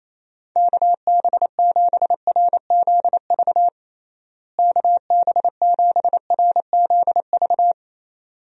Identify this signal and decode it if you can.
Morse code, heard as a keyed tone, about 28 wpm, 700 Hz. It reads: KB7RZV KB7RZV
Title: USA Call Signs - 6 Characters - Repeated 2x - 28wpm